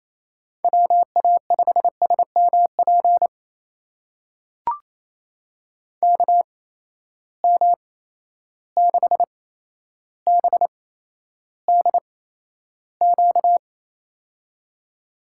Morse code, heard as a keyed tone, about 28 wpm, spaced as for 12 wpm, 700 Hz. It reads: WA5SMP KM6BDQ
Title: USA Call Signs - 6 Characters - 8x Character Spacing - 28wpm